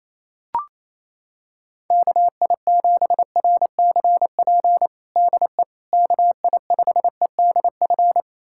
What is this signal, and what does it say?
Morse code, heard as a keyed tone, about 28 wpm, 700 Hz. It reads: KI7RCP DE KI5EDF
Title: USA Callsign DE Callsign 28wpm